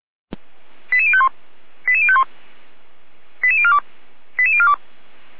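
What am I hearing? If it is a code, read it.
Five-tone selective call (ZVEI), heard as quick runs of five tones, tones of about 70 ms. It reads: 88051 88051 88042 88042
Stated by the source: ZVEI five-tone call 89104 received by SDR, archived by feuerwehr-orlamuende.org fire brigade.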